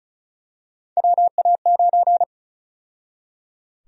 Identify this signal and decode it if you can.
Morse code, heard as a keyed tone, about 36 wpm, 700 Hz. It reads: WA9
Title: USA Call Sign Prefixes - 35wpm